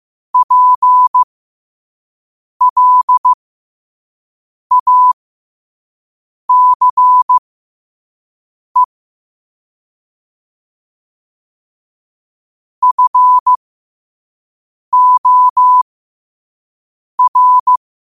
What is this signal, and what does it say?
Morse code, heard as a keyed tone, about 15 wpm, spaced as for 5 wpm, 1000 Hz. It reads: PLACE FOR
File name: qotdfiles / 5wpm